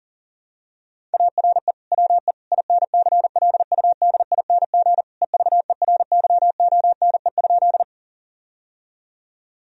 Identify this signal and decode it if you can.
Morse code, heard as a keyed tone, about 40 wpm, 700 Hz. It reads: ARE WE INCLUDING EVERYONE?
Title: Sentences from Top 1000 Words - Review Part 4 - 40wpm